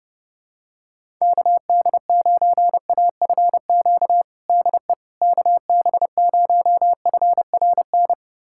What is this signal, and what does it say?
Morse code, heard as a keyed tone, about 32 wpm, 700 Hz. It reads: KD9AFQ DE KB0FRN
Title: USA Callsign DE Callsign - 6 Characters - 30wpm